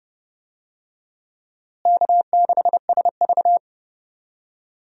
Morse code, heard as a keyed tone, about 31 wpm, 700 Hz. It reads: K6SV
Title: USA Call Signs - 4 Characters - Repeated 2x - 30wpm